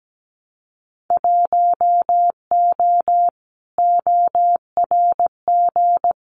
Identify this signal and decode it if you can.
Morse code, heard as a keyed tone, about 17 wpm, 700 Hz. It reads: 1O ORG
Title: Winter Field Day Exchanges 17wpm